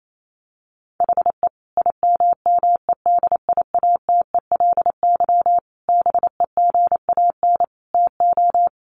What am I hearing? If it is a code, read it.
Morse code, heard as a keyed tone, about 28 wpm, 700 Hz. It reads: HE IMMEDIATELY BEGAN TO
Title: Sentences from Top 1100 Words 28wpm